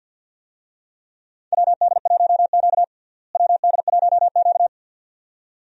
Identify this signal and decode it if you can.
Morse code, heard as a keyed tone, about 51 wpm, 700 Hz. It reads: WD1Q WD1Q
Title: USA Call Signs - 4 Characters - Repeated 2x - 50wpm